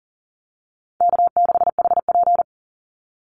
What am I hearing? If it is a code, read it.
Morse code, heard as a keyed tone, about 40 wpm, 700 Hz. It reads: K6HP
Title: Easy Call Signs Repeated 3x - 40wpm